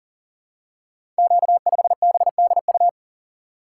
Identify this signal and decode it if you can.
Morse code, heard as a keyed tone, about 41 wpm, 700 Hz. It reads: Q5BDU
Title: Code Groups - 5 Letter:Numbers - 40wpm